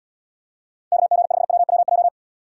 Morse code, heard as a keyed tone, about 78 wpm, 700 Hz. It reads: KG5WWJ